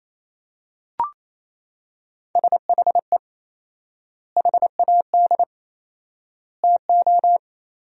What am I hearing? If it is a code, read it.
Morse code, heard as a keyed tone, about 28 wpm, 700 Hz. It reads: SHE HAD TO
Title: 4x Word-Spacing - Sentences from Top 200 Words - 28wpm